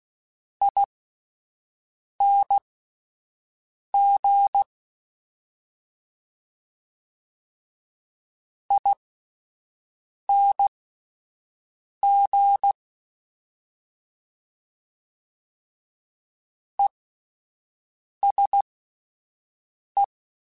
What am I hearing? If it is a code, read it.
Morse code, heard as a keyed tone, about 16 wpm, spaced as for 5 wpm, 800 Hz. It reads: ING ING ESE